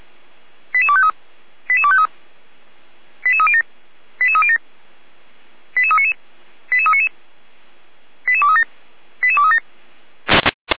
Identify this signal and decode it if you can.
Five-tone selective call (ZVEI), heard as quick runs of five tones, tones of about 70 ms. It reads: 88253 88253 88307 88307 88399 88399 88237 88237